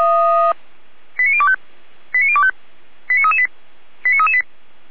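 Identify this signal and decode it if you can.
Five-tone selective call (ZVEI), heard as quick runs of five tones, tones of about 70 ms. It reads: 89025 89025 89338 89338
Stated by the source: ZVEI five-tone fire brigade alert for code 89508, received by SDR, from Feuerwehr Orlamünde.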